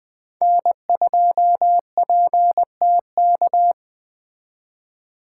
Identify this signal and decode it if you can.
Morse code, heard as a keyed tone, about 20 wpm, 700 Hz. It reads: N2PTK
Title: Speed Racing - USA 1x3 Call Signs 17wpm